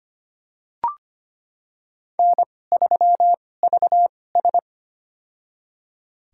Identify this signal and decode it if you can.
Morse code, heard as a keyed tone, about 25 wpm, spaced as for 18 wpm, 700 Hz. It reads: N3VS
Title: USA Call Signs - 2x Character Spacing - 25wpm